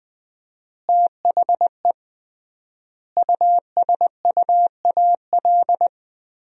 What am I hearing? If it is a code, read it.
Morse code, heard as a keyed tone, about 20 wpm, 700 Hz. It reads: THE USUAL
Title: 3x Word-Spacing - Sentences from Top 400 Words - 20wpm